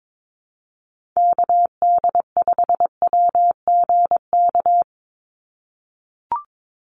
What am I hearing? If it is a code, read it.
Morse code, heard as a keyed tone, about 22 wpm, 700 Hz. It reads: KD5WGK
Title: USA Call Signs 22wpm